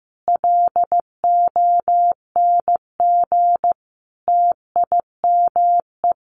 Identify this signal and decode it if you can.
Morse code, heard as a keyed tone, about 15 wpm, 700 Hz. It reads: LONG TIME
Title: Sets of 5 Words Encore2 15wpm